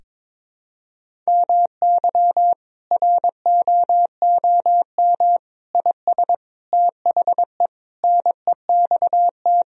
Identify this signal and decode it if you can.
Morse code, heard as a keyed tone, about 22 wpm, 700 Hz. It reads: MY ROOM IS THE NEXT
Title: Sentences from Top 1000 Words - Review Part 1 - 22wpm